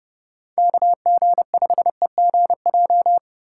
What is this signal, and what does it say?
Morse code, heard as a keyed tone, about 30 wpm, 690 Hz. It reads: KG5EGJ
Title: USA Call Signs - 3x Character Spacing - 30wpm